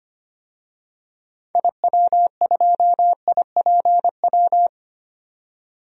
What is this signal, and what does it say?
Morse code, heard as a keyed tone, about 25 wpm, 700 Hz. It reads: IW2IPW
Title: DXCC-100 Call Signs - 6 Characters - 25wpm